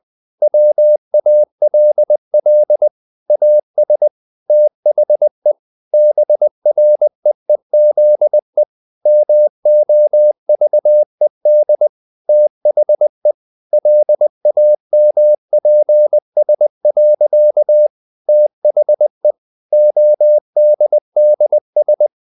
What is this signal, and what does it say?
Morse code, heard as a keyed tone, about 20 wpm, 600 Hz. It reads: WALL AS THE BREEZE MOVED THE LAMPS. THE ODDS
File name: the old man and the sea part 2 20wpm